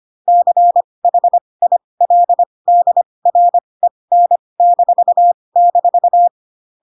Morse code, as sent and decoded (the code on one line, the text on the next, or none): -.-. .... .. .-.. -.. .-. -. -....- -....-
CHILDREN--